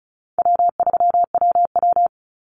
Morse code, heard as a keyed tone, about 34 wpm, 700 Hz. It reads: W3WW
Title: Call Signs Repeated 3x - 35wpm